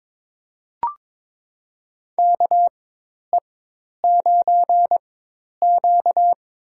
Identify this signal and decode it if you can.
Morse code, heard as a keyed tone, about 22 wpm, spaced as for 10 wpm, 700 Hz. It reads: KE9Q
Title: USA Call Signs - 4 Characters - 4x Character Spacing - 22wpm